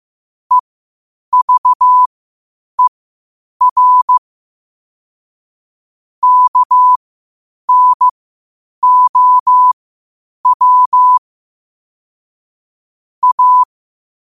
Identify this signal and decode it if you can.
Morse code, heard as a keyed tone, about 15 wpm, spaced as for 8 wpm, 1000 Hz. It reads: EVER KNOW A
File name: qotdfiles / 8wpm